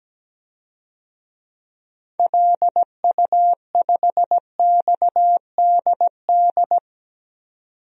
Morse code, heard as a keyed tone, about 17 wpm, 700 Hz. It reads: LU5XDD
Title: DXCC-100 Call Signs - 6 Characters - 17wpm